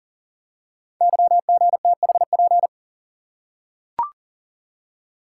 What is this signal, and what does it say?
Morse code, heard as a keyed tone, about 40 wpm, 700 Hz. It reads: YGTHP
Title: Code Groups - 5 Letters - 40wpm